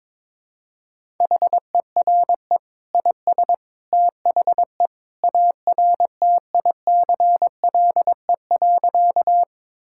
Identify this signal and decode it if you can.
Morse code, heard as a keyed tone, about 22 wpm, 700 Hz. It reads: HERE IS THE ARTICLE.
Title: Sentences from Top 1000 Words - Review Part 5 - 22wpm